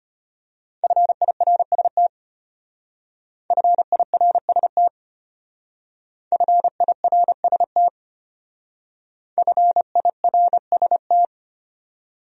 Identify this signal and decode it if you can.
Morse code, heard as a keyed tone, about 31 wpm, 700 Hz. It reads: FIRST FIRST FIRST FIRST
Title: Speed-Racing Top 100 Words 25wpm